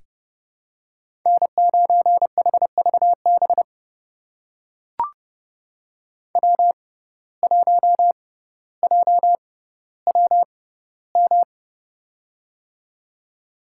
Morse code, heard as a keyed tone, about 30 wpm, spaced as for 15 wpm, 700 Hz. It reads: N9HVB W1JWM
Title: USA Call Signs - 6x Character Spacing - 30wpm